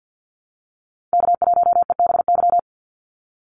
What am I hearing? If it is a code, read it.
Morse code, heard as a keyed tone, about 50 wpm, 700 Hz. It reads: K1EBY
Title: Call Signs Repeated 2x - 50wpm